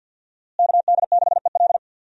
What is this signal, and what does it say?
Morse code, heard as a keyed tone, about 45 wpm, 700 Hz. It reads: KD6EL
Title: USA Call Signs - 5 Characters - Repeated 2x - 50wpm